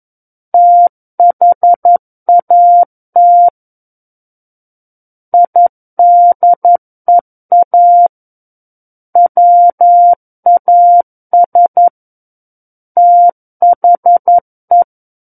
Morse code, heard as a keyed tone, about 11 wpm, 700 Hz. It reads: THAT IDEA WAS THE
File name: Book One / 02-Chapter 2 - 11 WPM